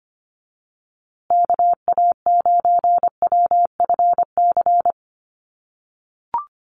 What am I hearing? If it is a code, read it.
Morse code, heard as a keyed tone, about 25 wpm, 700 Hz. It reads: KA9WFC
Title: USA Call Signs Repeated 2x 25wpm